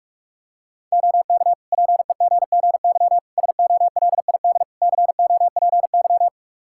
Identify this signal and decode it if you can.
Morse code, heard as a keyed tone, about 45 wpm, 700 Hz. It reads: OK PEGGY SOLID COPY
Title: Practice QSO - Exchange 3 - More Info - 45wpm